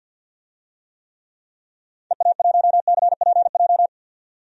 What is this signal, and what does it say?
Morse code, heard as a keyed tone, about 46 wpm, 700 Hz. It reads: EA1CPJ
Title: DXCC-100 Call Signs - 6 Characters - 50wpm